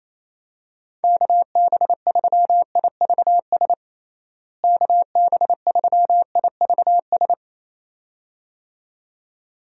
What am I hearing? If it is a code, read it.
Morse code, heard as a keyed tone, about 28 wpm, 700 Hz. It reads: KB3IVS KB3IVS
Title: USA Call Signs - 6 Characters - Repeated 2x - 28wpm